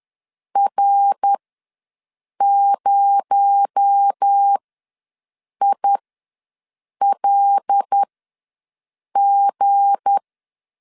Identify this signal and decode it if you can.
Morse code, heard as a keyed tone, about 10 wpm, spaced as for 6 wpm, 800 Hz. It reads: R0ILG
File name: LEZIONI / 7 GF9Z / 7 G ws 10wpm 3843